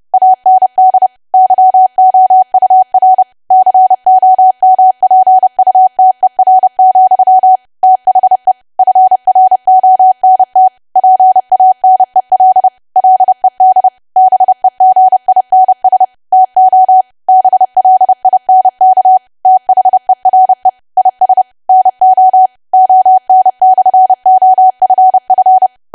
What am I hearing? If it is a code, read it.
Morse code, heard as a keyed tone, about 30 wpm, 750 Hz. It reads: AND YOUR COMPUTER, THE FRONT PANEL LED BEGINS TO BLINK THERE IS NO ON/OFF
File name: cwa / W1AW-Level3 / 30WPM